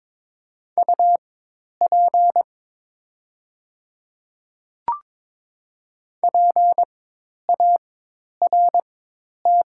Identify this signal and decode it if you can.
Morse code, heard as a keyed tone, about 22 wpm, spaced as for 10 wpm, 700 Hz. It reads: UP PART
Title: 4x Character Spacing - Top 200 Words - 22wpm